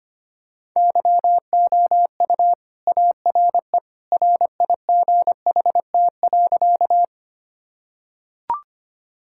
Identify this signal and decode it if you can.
Morse code, heard as a keyed tone, about 25 wpm, 700 Hz. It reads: YOU ARE RIGHT.
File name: practice / 25wpm / Sentences with 2-Word Phrases - 25wpm